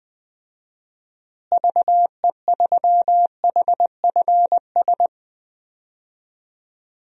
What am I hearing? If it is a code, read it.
Morse code, heard as a keyed tone, about 20 wpm, 700 Hz. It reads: VE3HFS
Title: DXCC-100 Call Signs - 6 Characters - Repeated 3x - 20wpm